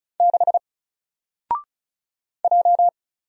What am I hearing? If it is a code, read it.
Morse code, heard as a keyed tone, about 36 wpm, 700 Hz. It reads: 6 J
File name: Single-Letter-Number-35wpm